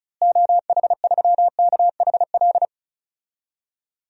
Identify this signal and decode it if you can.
Morse code, heard as a keyed tone, about 36 wpm, 700 Hz. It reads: OH3KHL
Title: DXCC-100 Call Signs - 6 Characters - Repeated 3x - 35wpm